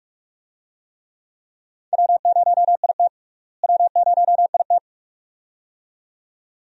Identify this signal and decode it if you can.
Morse code, heard as a keyed tone, about 45 wpm, 700 Hz. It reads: W0IT W0IT
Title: USA Call Signs - 4 Characters - Repeated 2x - 45wpm